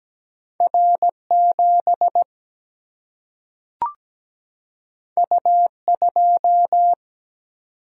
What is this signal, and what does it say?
Morse code, heard as a keyed tone, about 17 wpm, 700 Hz. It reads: R 7 U 2
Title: Single-Letter Single-Number 17wpm